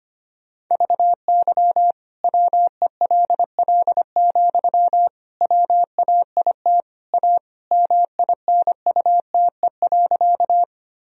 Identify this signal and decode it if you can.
Morse code, heard as a keyed tone, about 25 wpm, 700 Hz. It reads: VY WELL, WAIT A MINUTE.
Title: Sentences from Top 500 Words and Abbreviations 25wpm